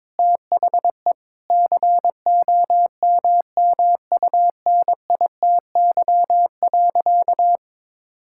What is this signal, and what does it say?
Morse code, heard as a keyed tone, about 21 wpm, 700 Hz. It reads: THE COMMUNITY.
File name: practice / 22wpm / Sentences with 8-Word Phrases - 22wpm